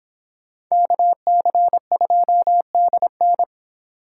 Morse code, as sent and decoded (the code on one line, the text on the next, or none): -.- -.-. ..--- -.. -.
KC2DN